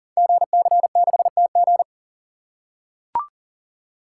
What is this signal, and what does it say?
Morse code, heard as a keyed tone, about 42 wpm, 690 Hz. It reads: GC6TG
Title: Code Groups - 5 Letter:Numbers - 40wpm